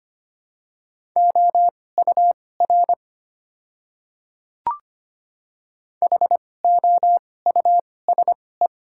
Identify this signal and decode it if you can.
Morse code, heard as a keyed tone, about 25 wpm, spaced as for 18 wpm, 700 Hz. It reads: OUR HOUSE
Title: 2x Character Spacing - Top 200 Words - 25wpm